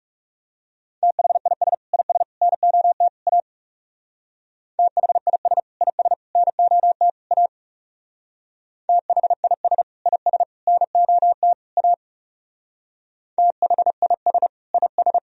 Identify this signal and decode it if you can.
Morse code, heard as a keyed tone, about 37 wpm, 700 Hz. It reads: THIS IS NOT A THIS IS NOT A THIS IS NOT A THIS IS
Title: Speed-Racing - Sets of 4 Words 30wpm